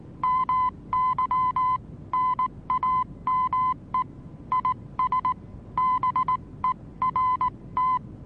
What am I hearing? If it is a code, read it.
Morse code, heard as a keyed tone, about 19 wpm, 1020 Hz. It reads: MY NAME IS BERT
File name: repos / Morsekurs / DJ4UF Morsen26